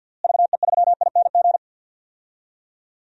Morse code, heard as a keyed tone, about 51 wpm, 700 Hz. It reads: VE3ING
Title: DXCC-100 Call Signs - 6 Characters - Repeated 3x - 50wpm